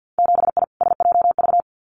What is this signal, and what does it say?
Morse code, heard as a keyed tone, about 50 wpm, 700 Hz. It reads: 7I SJV
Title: Winter Field Day Exchanges Encore 50wpm